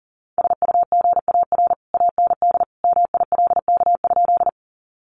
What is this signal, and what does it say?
Morse code, heard as a keyed tone, about 40 wpm, 700 Hz. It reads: SUGAR AND MILK?
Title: Sentences from Top 1500 Words 40wpm